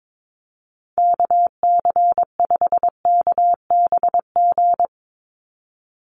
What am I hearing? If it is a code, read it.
Morse code, heard as a keyed tone, about 22 wpm, 700 Hz. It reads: KC5KBG
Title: USA Call Signs 22wpm